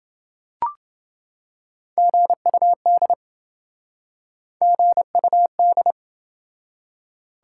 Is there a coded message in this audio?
Yes: Morse code from 2.0 to 5.9 s, heard as a keyed tone, about 27 wpm, 700 Hz. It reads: GUD GUD